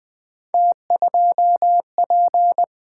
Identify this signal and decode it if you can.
Morse code, heard as a keyed tone, about 20 wpm, 700 Hz. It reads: T2P
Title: Three Letter-Number Combinations - 20wpm